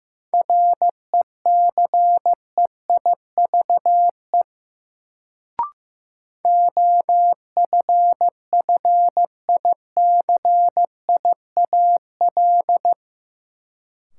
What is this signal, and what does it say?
Morse code, heard as a keyed tone, about 15 wpm, 700 Hz. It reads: RECEIVE OFFICIAL
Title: Sentences from Top 1000 Words - New Word Drill - 15wpm